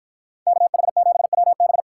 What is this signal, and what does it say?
Morse code, heard as a keyed tone, about 51 wpm, 700 Hz. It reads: KH7WB